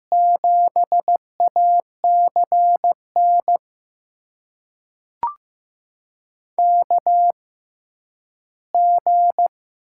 Morse code, heard as a keyed tone, about 15 wpm, spaced as for 10 wpm, 700 Hz. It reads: 7ACN KG
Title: USA Call Signs - 6 Characters - 6x Character Spacing - 15wpm